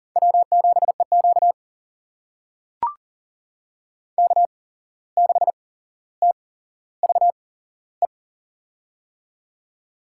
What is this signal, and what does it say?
Morse code, heard as a keyed tone, about 40 wpm, spaced as for 16 wpm, 700 Hz. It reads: W7EQ K6TVE
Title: USA Call Signs - 8x Character Spacing - 40wpm